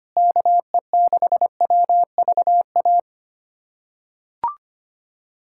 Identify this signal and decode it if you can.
Morse code, heard as a keyed tone, about 25 wpm, 700 Hz. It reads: KE6WVA
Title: USA Callsign DE Callsign - 6 Characters - 25wpm